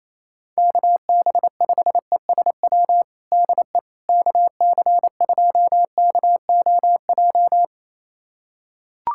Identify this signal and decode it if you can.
Morse code, heard as a keyed tone, about 28 wpm, 700 Hz. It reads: KB5ESW DE KC2KOJ
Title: USA Callsign DE Callsign 28wpm